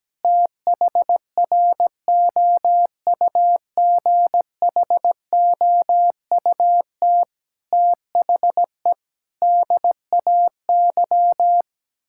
Morse code, heard as a keyed tone, about 17 wpm, 700 Hz. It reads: THROUGHOUT THE DAY
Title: Sentences from Top 1000 Words - Review Part 6 - 17wpm